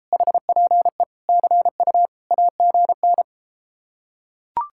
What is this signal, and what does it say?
Morse code, heard as a keyed tone, about 33 wpm, 700 Hz. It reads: HPE CU AGN